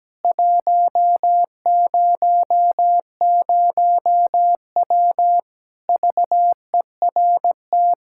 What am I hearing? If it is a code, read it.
Morse code, heard as a keyed tone, about 17 wpm, 700 Hz. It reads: 100W VERT
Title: K1USN SST QSO Practice 17wpm